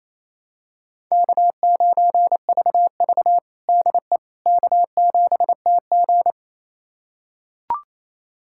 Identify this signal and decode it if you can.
Morse code, heard as a keyed tone, about 28 wpm, 700 Hz. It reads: K9VV DE K7TG
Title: USA Callsign DE Callsign - 4 Characters - 28wpm